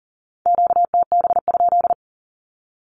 Morse code, heard as a keyed tone, about 40 wpm, 700 Hz. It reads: QTB?
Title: Q-Codes with Spoken Meaning - 40wpm